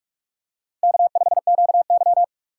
Morse code, heard as a keyed tone, about 44 wpm, 700 Hz. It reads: K5QY
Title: USA Call Signs - 6x Character Spacing - 45wpm